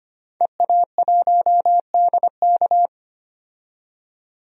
Morse code, heard as a keyed tone, about 25 wpm, 700 Hz. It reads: EA1DK
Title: DXCC-100 Call Signs - 5 Characters - 25wpm